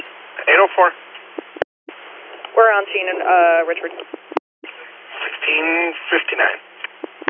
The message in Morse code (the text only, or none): none